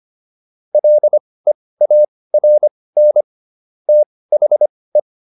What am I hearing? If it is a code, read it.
Morse code, heard as a keyed tone, about 25 wpm, spaced as for 18 wpm, 600 Hz. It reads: LEARN THE